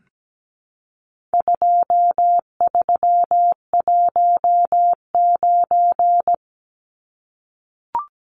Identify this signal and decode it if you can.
Morse code, heard as a keyed tone, about 17 wpm, 700 Hz. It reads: 2319
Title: Numbers - 4-Digit 17wpm